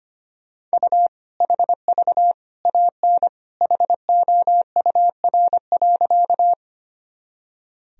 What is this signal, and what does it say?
Morse code, heard as a keyed tone, about 25 wpm, 700 Hz. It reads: U HV AN HOUR.
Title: Sentences from Top 400 Words and Abbreviations 25wpm